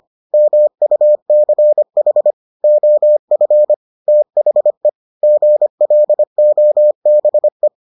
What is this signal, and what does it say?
Morse code, heard as a keyed tone, about 25 wpm, 600 Hz. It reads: MUCH OF THE GLOBE